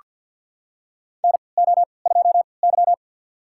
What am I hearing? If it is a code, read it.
Morse code, heard as a keyed tone, about 48 wpm, spaced as for 28 wpm, 700 Hz. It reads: NO2Y
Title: USA Call Signs - 4 Characters - 3x Character Spacing - 50wpm